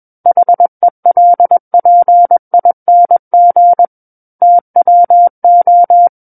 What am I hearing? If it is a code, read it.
Morse code, heard as a keyed tone, about 21 wpm, 700 Hz. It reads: HELPING TWO
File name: Book One / 12-Chapter 12 - 21 WPM